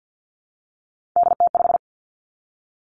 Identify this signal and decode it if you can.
Morse code, heard as a keyed tone, about 50 wpm, 700 Hz. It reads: DT5
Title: Three Challenging Letter-Number Combinations 50wpm